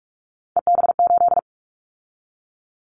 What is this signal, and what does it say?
Morse code, heard as a keyed tone, about 46 wpm, 700 Hz. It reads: EB8
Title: Callsign Prefixes for Top 50 DXCC - 45wpm